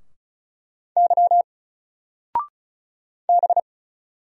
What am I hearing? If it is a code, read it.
Morse code, heard as a keyed tone, about 36 wpm, 700 Hz. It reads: Y B
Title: Single Letter-Number 35wpm